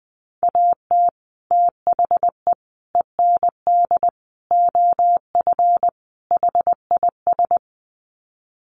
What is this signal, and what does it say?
Morse code, heard as a keyed tone, about 20 wpm, 700 Hz. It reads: AT THE END OF HIS
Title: Sets of 5 Words Encore 20wpm